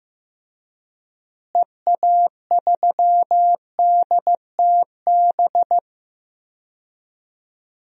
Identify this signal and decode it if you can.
Morse code, heard as a keyed tone, about 16 wpm, 700 Hz. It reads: EA3DTB
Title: DXCC-100 Call Signs - 6 Characters - Repeated 2x - 15wpm